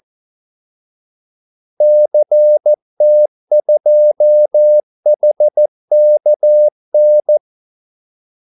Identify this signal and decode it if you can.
Morse code, heard as a keyed tone, about 14 wpm, 600 Hz. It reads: CT2HKN
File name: Callsigns-14wpm